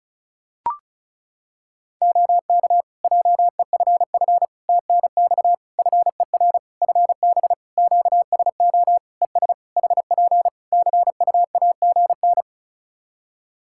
Morse code, heard as a keyed tone, about 36 wpm, 700 Hz. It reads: OK JEFF TNX FER FB QSO ES HP CUAGN
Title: Practice QSO - Exchange 4 - Ending - 35wpm